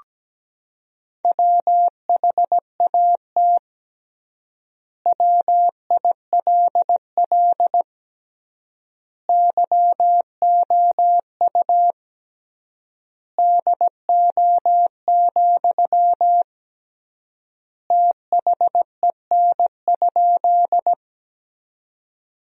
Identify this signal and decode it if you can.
Morse code, heard as a keyed tone, about 17 wpm, 700 Hz. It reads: WHAT WILL YOU DO, THEN?
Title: 3x Word-Spacing - Sentences from Top 100 Words - 17wpm